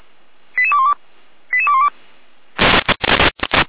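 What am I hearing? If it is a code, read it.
Five-tone selective call (ZVEI), heard as quick runs of five tones, tones of about 70 ms. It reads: 88212 88212
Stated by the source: ZVEI five-tone call received by SDR.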